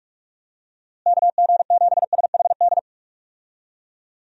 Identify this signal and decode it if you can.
Morse code, heard as a keyed tone, about 42 wpm, 700 Hz. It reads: KG7SHD
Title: USA Call Signs - 6 Characters - Repeated 2x - 45wpm